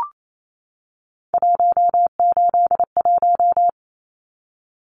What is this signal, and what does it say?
Morse code, heard as a keyed tone, about 28 wpm, 700 Hz. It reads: 181
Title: Numbers - 3-Digit 28wpm